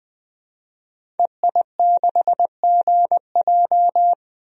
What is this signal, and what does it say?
Morse code, heard as a keyed tone, about 21 wpm, 700 Hz. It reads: EI6GJ
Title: DXCC-100 Call Signs - 5 Characters - 20wpm